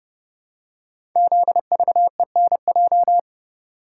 Morse code, heard as a keyed tone, about 30 wpm, 700 Hz. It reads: ZVENJ
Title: Code Groups - 5 Letters - 30wpm